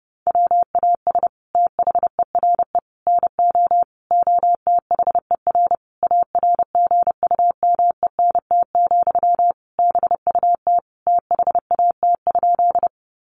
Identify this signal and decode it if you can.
Morse code, heard as a keyed tone, about 30 wpm, 700 Hz. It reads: WAS THERE NO OTHER ARGUMENT, BUT THAT?
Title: Sentences from Top 1400 Words 30wpm